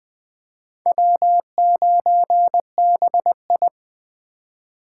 Morse code, heard as a keyed tone, about 20 wpm, 700 Hz. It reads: W9BI
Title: USA Call Signs - 4 Characters - 4x Character Spacing - 20wpm